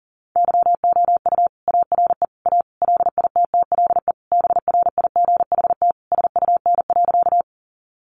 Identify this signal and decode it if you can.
Morse code, heard as a keyed tone, about 40 wpm, 700 Hz. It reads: YOU ARE A LITTLE BRIGHT SUN.
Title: Sentences from Top 1000 Words 40wpm